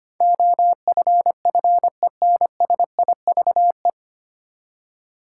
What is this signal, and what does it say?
Morse code, heard as a keyed tone, about 25 wpm, 700 Hz. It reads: OFFENSIVE
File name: Nine Letter Words - 25wpm